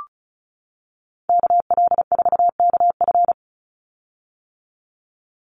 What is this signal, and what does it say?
Morse code, heard as a keyed tone, about 34 wpm, 700 Hz. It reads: KL4KF
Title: USA Call Signs 35wpm